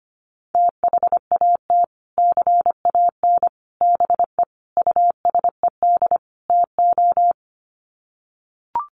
Morse code, heard as a keyed tone, about 25 wpm, 700 Hz. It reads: THAT CAN BE USED TO
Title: Sets of 5 Words Encore 25wpm